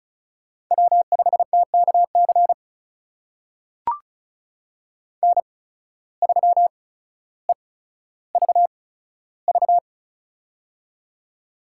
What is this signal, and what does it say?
Morse code, heard as a keyed tone, about 36 wpm, spaced as for 15 wpm, 700 Hz. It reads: W5TKC N3EVV
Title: USA Call Signs - 5 Characters - 8x Character Spacing - 35wpm